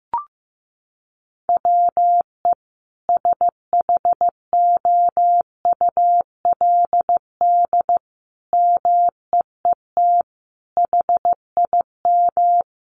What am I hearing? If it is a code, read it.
Morse code, heard as a keyed tone, about 15 wpm, 700 Hz. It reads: WE SHOULD MEET HIM
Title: Sentences from Top 1100 Words 15wpm